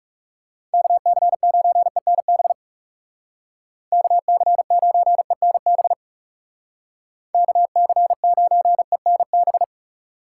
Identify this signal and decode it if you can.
Morse code, heard as a keyed tone, about 37 wpm, 700 Hz. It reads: KC9ENB KC9ENB KC9ENB
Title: Speed Racing - USA 2x3 Call Signs 30wpm